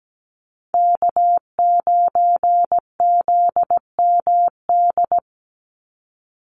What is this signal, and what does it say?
Morse code, heard as a keyed tone, about 17 wpm, 700 Hz. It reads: K9ZMD
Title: NA SOTA Chaser Callsigns 17wpm